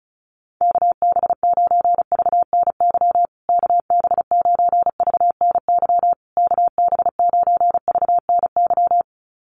Morse code, heard as a keyed tone, about 34 wpm, 700 Hz. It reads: KB9VNY KB9VNY KB9VNY
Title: USA Call Signs Repeated 3x 35wpm